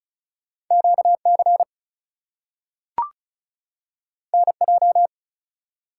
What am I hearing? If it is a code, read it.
Morse code, heard as a keyed tone, about 36 wpm, 700 Hz. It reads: QC NJ
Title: US States and Canadian Provinces 35wpm